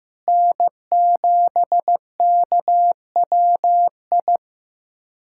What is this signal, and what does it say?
Morse code, heard as a keyed tone, about 15 wpm, 700 Hz. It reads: N7KWI
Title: USA Callsign DE Callsign 15wpm